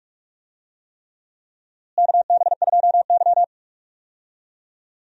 Morse code, heard as a keyed tone, about 46 wpm, 700 Hz. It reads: KB2Y
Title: USA Call Signs - 4 Characters - Repeated 3x - 45wpm